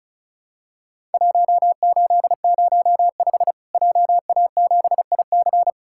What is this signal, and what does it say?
Morse code, heard as a keyed tone, about 36 wpm, 700 Hz. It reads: 1805 JA7IC
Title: Worked All Europe DX Contest - QTC Exchange - 35wpm